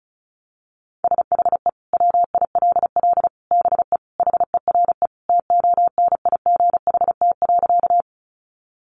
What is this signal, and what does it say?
Morse code, heard as a keyed tone, about 34 wpm, 700 Hz. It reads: SHE WILL BE HERE TONIGHT.
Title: Sentences from Top 600 Words 35wpm